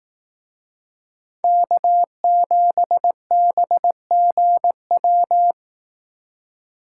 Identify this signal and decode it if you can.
Morse code, heard as a keyed tone, about 18 wpm, 700 Hz. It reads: K7BGW
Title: Speed Racing - USA 1x3 Call Signs 15wpm